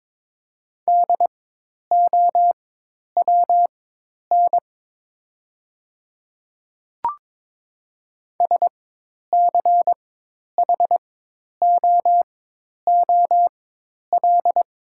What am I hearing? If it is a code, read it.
Morse code, heard as a keyed tone, about 22 wpm, spaced as for 10 wpm, 700 Hz. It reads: DOWN SCHOOL
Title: 4x Character Spacing - Top 300 Words - 22wpm